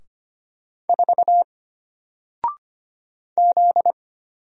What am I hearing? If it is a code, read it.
Morse code, heard as a keyed tone, about 25 wpm, 700 Hz. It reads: 4 Z
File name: Single-Letter-Number-25wpm